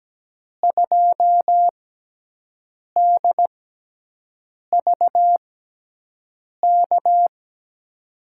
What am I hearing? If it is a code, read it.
Morse code, heard as a keyed tone, about 17 wpm, spaced as for 6 wpm, 700 Hz. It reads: 2DVK
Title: USA Call Signs - 6 Characters - 6x Character Spacing - 17wpm